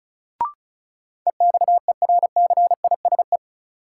Morse code, heard as a keyed tone, about 34 wpm, 700 Hz. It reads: EXERCISE
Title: Sentences from Top 1000 Words - Review Part 4 - New Word Drill - 35wpm